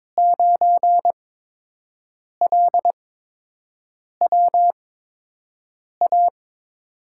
Morse code, heard as a keyed tone, about 21 wpm, spaced as for 6 wpm, 700 Hz. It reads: 9LWA
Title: USA Call Signs - 8x Character Spacing - 22wpm